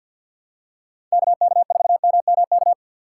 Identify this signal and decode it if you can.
Morse code, heard as a keyed tone, about 47 wpm, 690 Hz. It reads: KK4MMK
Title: USA Call Signs - 6x Character Spacing - 50wpm